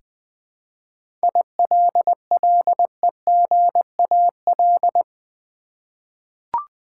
Morse code, heard as a keyed tone, about 20 wpm, 700 Hz. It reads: ILLEGAL
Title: Seven Letter Words - 20wpm